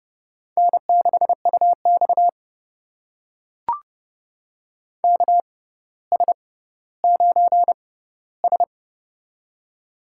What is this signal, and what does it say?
Morse code, heard as a keyed tone, about 30 wpm, spaced as for 15 wpm, 700 Hz. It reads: N6UX KS9S
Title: USA Call Signs - 4 Characters - 6x Character Spacing - 30wpm